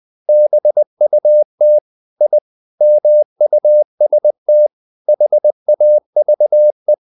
Morse code, heard as a keyed tone, about 20 wpm, 600 Hz. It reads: BUT I MUST HAVE
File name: the old man and the sea / the old man and the sea part 1 20wpm